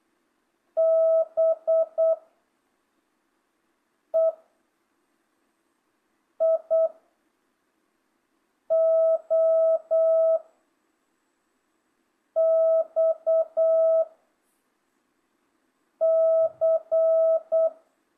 Morse code, heard as a keyed tone, about 8 wpm, 660 Hz. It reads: B E I O X C
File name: jh8cbh / 2025cw